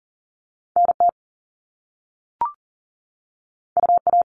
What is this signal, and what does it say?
Morse code, heard as a keyed tone, about 42 wpm, 700 Hz. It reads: NT UA
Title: Top Two Letter Combinations in Words 40wpm